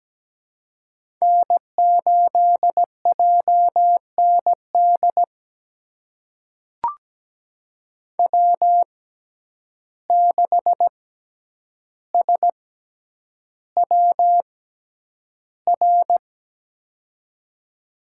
Morse code, heard as a keyed tone, about 17 wpm, spaced as for 9 wpm, 700 Hz. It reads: N8JND W6SWR
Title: USA Call Signs - 5 Characters - 6x Character Spacing - 17wpm